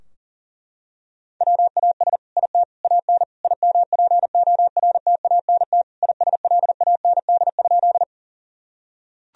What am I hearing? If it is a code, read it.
Morse code, heard as a keyed tone, about 40 wpm, 700 Hz. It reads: WAS IT AN IMPORTANT ISLAND?
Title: Sentences from Top 1000 Words - Review Part 3 - 40wpm